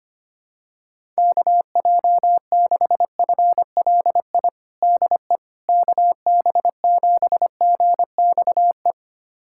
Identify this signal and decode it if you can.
Morse code, heard as a keyed tone, about 25 wpm, 700 Hz. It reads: KJ6FLI DE KB7GXE
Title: USA Callsign DE Callsign - 6 Characters - 25wpm